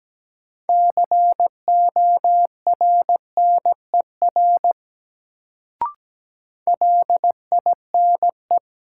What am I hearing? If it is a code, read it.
Morse code, heard as a keyed tone, about 17 wpm, 700 Hz. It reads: CORNER LINE
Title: Sets of 5 Words - New Word Drill - 17wpm